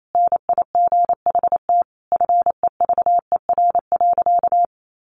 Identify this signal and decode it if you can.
Morse code, heard as a keyed tone, about 28 wpm, 700 Hz. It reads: NIGHT FEVER.
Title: A Sound Of Thunder 28wpm